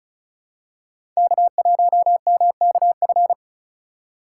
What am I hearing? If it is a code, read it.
Morse code, heard as a keyed tone, about 36 wpm, 700 Hz. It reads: K1MKF
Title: USA Call Signs - 5 Characters - 4x Character Spacing - 35wpm